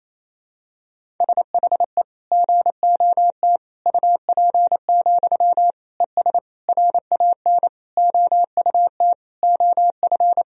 Morse code, heard as a keyed tone, about 28 wpm, 700 Hz. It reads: SHE GOT UP, ES RAN OUT OF
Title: Sentences from Top 500 Words and Abbreviations 28wpm